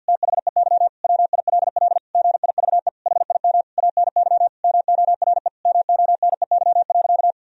Morse code, heard as a keyed tone, about 49 wpm, 700 Hz. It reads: THEY WILL GIVE HIM ANY MORE MONEY.
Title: Sentences from Top 1000 Words - Review Part 4 - 50wpm